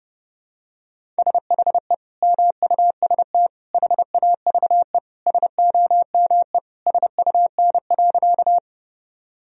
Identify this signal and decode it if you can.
Morse code, heard as a keyed tone, about 32 wpm, 700 Hz. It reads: SHE MUST HAVE SOME SUN.
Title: Sentences from Top 1000 Words - Review Part 1 - 30wpm